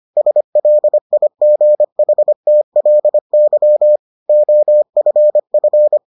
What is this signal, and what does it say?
Morse code, heard as a keyed tone, about 25 wpm, 600 Hz. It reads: SLIGHTLY OFF